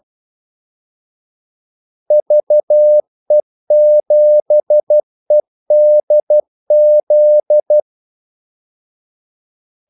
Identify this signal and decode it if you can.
Morse code, heard as a keyed tone, about 12 wpm, 600 Hz. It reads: VE7EDZ
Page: Kaldesignaler 12 wpm